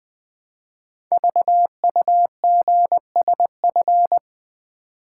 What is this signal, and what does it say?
Morse code, heard as a keyed tone, about 20 wpm, 700 Hz. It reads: VUGSF
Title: Code Groups - 5 Letters - 20wpm